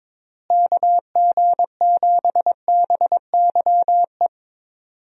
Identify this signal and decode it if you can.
Morse code, heard as a keyed tone, about 22 wpm, 700 Hz. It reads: KG7BYE
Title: USA Call Signs - 6 Characters - Repeated 2x - 22wpm